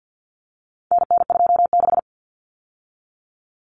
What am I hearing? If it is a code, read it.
Morse code, heard as a keyed tone, about 50 wpm, 700 Hz. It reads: NN26
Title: Common Cut Numbers - 4-Digit 50wpm